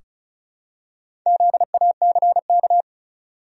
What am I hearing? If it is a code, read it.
Morse code, heard as a keyed tone, about 36 wpm, 700 Hz. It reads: ZACK
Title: USA Common Names - Max 4 Letters - 35wpm